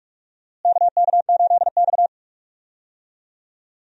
Morse code, heard as a keyed tone, about 42 wpm, 700 Hz. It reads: KK8X
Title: USA Call Signs - 4 Characters - Repeated 2x - 45wpm